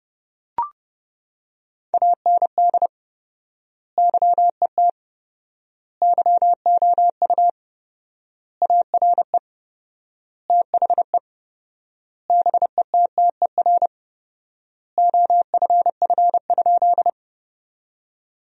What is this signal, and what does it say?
Morse code, heard as a keyed tone, about 30 wpm, 700 Hz. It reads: AND YET YOU ARE THE BETTER OFF?
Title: 4x Word-Spacing - Sentences from Top 500 Words - 30wpm